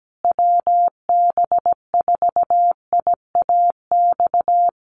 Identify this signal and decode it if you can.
Morse code, heard as a keyed tone, about 17 wpm, 700 Hz. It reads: WB4IAX
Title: USA Call Signs Repeated 3x 17wpm